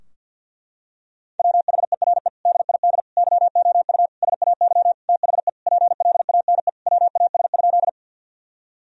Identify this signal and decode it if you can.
Morse code, heard as a keyed tone, about 50 wpm, 700 Hz. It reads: WHERE DID YOU SAY THE PLANE WAS?
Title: Sentences from Top 1000 Words - Review Part 3 - 50wpm